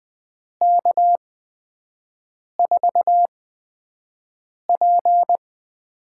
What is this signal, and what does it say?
Morse code, heard as a keyed tone, about 20 wpm, spaced as for 5 wpm, 700 Hz. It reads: K4P
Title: USA Call Signs - 6 Characters - 8x Character Spacing - 20wpm